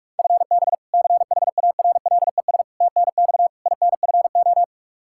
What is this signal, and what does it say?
Morse code, heard as a keyed tone, about 46 wpm, 700 Hz. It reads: FB CHARLES TNX INFO
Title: Practice QSO - Exchange 3 - More Info - 45wpm